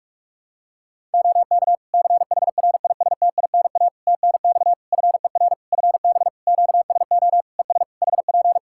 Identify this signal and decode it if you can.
Morse code, heard as a keyed tone, about 46 wpm, 700 Hz. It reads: OK CHRISTINA TNX FER FB QSO ES HP
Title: Practice QSO - Exchange 4 - Ending - 45wpm